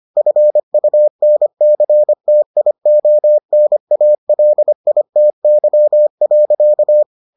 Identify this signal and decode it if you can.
Morse code, heard as a keyed tone, about 25 wpm, 600 Hz. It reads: FUNCTIONALITY.